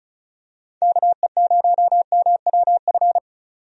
Morse code, heard as a keyed tone, about 34 wpm, 700 Hz. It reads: KE0MWF
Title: USA Call Signs - 6 Characters - 35wpm